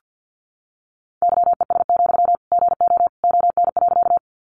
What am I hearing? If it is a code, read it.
Morse code, heard as a keyed tone, about 50 wpm, 700 Hz. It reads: YES, GO ON.
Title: Sentences from Top 500 Words 50wpm